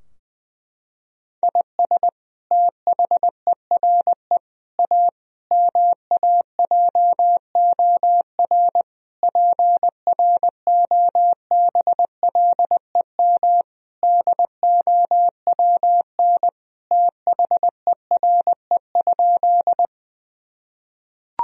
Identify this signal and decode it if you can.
Morse code, heard as a keyed tone, about 20 wpm, 700 Hz. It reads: IS THERE A MAJOR PROBLEM DOWN THERE?
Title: Sentences from Top 1000 Words - Review Part 4 - 20wpm